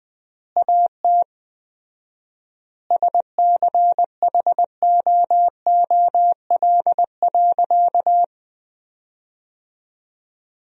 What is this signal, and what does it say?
Morse code, heard as a keyed tone, about 21 wpm, 700 Hz. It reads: AT SCHOOL.
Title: 4x Word-Spacing - Sentences from Top 300 Words - 20wpm